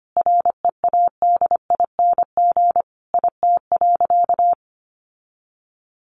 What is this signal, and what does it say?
Morse code, heard as a keyed tone, about 25 wpm, 700 Hz. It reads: READING IT.
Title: Sentences from Top 1200 Words 25wpm